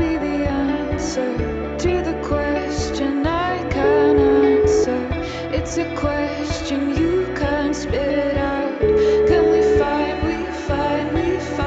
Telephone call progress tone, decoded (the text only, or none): ringback tone